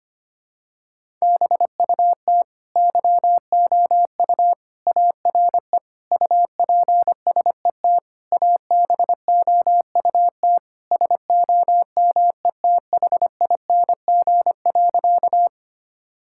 Morse code, heard as a keyed tone, about 25 wpm, 700 Hz. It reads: BUT YOU ARE UPSET ABOUT SOMETHING.
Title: Sentences from Top 1000 Words - Review Part 5 - 25wpm